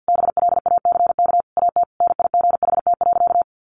Morse code, heard as a keyed tone, about 50 wpm, 700 Hz. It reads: BLACK AT NIGHT.
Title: Sentences from Top 1300 Words 50wpm